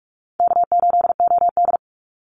Morse code, heard as a keyed tone, about 44 wpm, 700 Hz. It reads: K8OD
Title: Call Signs - 45wpm